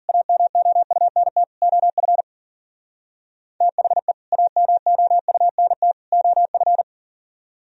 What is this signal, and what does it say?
Morse code, heard as a keyed tone, about 49 wpm, 700 Hz. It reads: AMOUNT OF THE AMOUNT OF
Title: Speed-Racing - Sets of 3 Words 40wpm